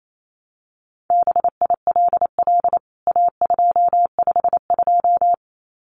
Morse code, heard as a keyed tone, about 28 wpm, 700 Hz. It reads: BILL A252
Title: CWT Exchange 28wpm